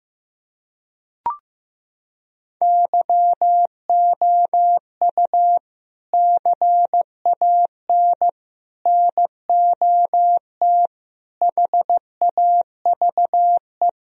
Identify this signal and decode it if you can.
Morse code, heard as a keyed tone, about 15 wpm, 700 Hz. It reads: YOU CAN NOT HAVE
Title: Sentences from Top 1000 Words - Review Part 4 - 15wpm